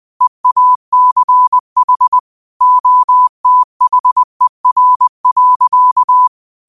Morse code, heard as a keyed tone, about 20 wpm, 1000 Hz. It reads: EACH OTHER.
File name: 20wpm